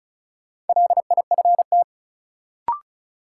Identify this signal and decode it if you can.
Morse code, heard as a keyed tone, about 35 wpm, 700 Hz. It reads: LIFT
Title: Sentences from Top 1000 Words - New Word Drill - 35wpm